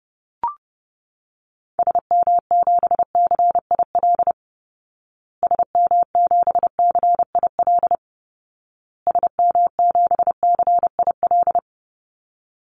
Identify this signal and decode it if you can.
Morse code, heard as a keyed tone, about 30 wpm, 700 Hz. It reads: SM7CIL SM7CIL SM7CIL
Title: Call Signs Repeated 2x - 30wpm